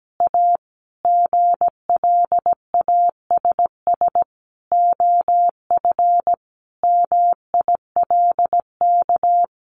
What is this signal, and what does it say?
Morse code, heard as a keyed tone, about 17 wpm, 700 Hz. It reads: A GLASS OF MILK
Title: Additional Easy Phrases 17wpm